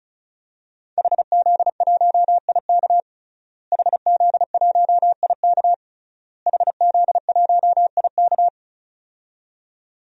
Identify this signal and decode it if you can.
Morse code, heard as a keyed tone, about 35 wpm, 700 Hz. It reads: HZ1IK HZ1IK HZ1IK
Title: DXCC-100 Call Signs - 5 Characters - Repeated 3x - 35wpm